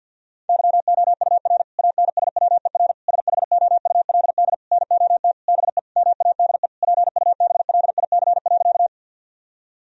Morse code, heard as a keyed tone, about 50 wpm, 700 Hz. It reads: YOUR ANSWER SHOULD NOT BE MADE PUBLIC.
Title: Sentences from Top 1000 Words - Review Part 3 - 50wpm